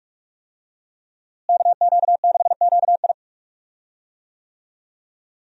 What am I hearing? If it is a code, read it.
Morse code, heard as a keyed tone, about 45 wpm, 700 Hz. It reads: KQ6QI
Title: USA Call Signs - 5 Characters - Repeated 2x - 45wpm